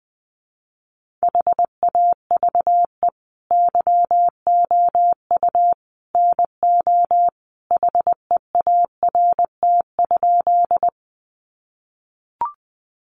Morse code, heard as a keyed tone, about 20 wpm, 700 Hz. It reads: HAVE YOU NO HEART?
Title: Sentences from Top 1200 Words 20wpm